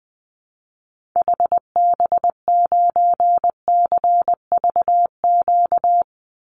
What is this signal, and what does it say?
Morse code, heard as a keyed tone, about 20 wpm, 700 Hz. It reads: HB9CVQ
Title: Call Signs Repeated 2x - 20wpm